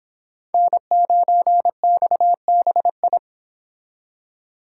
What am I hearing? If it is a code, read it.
Morse code, heard as a keyed tone, about 26 wpm, 700 Hz. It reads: N9XBI